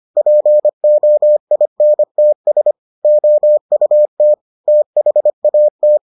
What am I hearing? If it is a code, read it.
Morse code, heard as a keyed tone, about 25 wpm, 600 Hz. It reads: POINTS OUT THAT